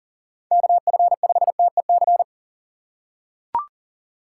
Morse code, heard as a keyed tone, about 37 wpm, 700 Hz. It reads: KF5TEC
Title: USA Call Signs - 6 Characters - 6x Character Spacing - 40wpm